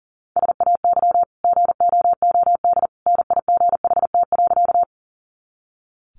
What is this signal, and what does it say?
Morse code, heard as a keyed tone, about 40 wpm, 700 Hz. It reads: SAY GOOD NIGHT.
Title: Sentences from Top 300 Words 40wpm